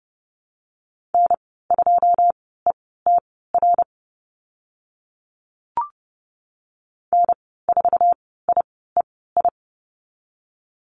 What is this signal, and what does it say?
Morse code, heard as a keyed tone, about 30 wpm, spaced as for 17 wpm, 700 Hz. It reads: N2ETR N4IEI
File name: course / 30:10wpm / 088 Lesson 15g - Number 5 - Call Signs 30wpm